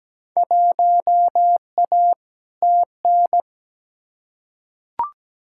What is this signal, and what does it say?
Morse code, heard as a keyed tone, about 17 wpm, 700 Hz. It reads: 1A TN
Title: Field Day Exchanges - 17wpm